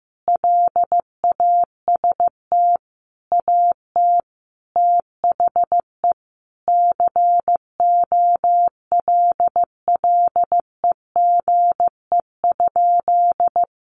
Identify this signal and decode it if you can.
Morse code, heard as a keyed tone, about 15 wpm, 700 Hz. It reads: LAST AT THE COLLEGE?
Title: Sentences from Top 1300 Words 15wpm